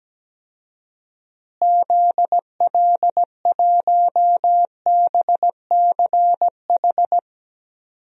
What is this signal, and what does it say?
Morse code, heard as a keyed tone, about 17 wpm, 700 Hz. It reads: ZL1BCH
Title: DXCC-100 Call Signs - 6 Characters - Repeated 2x - 17wpm